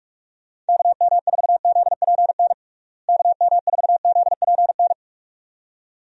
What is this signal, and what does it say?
Morse code, heard as a keyed tone, about 48 wpm, 700 Hz. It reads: KM4ZPN KM4ZPN
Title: USA Call Signs - 6 Characters - Repeated 2x - 45wpm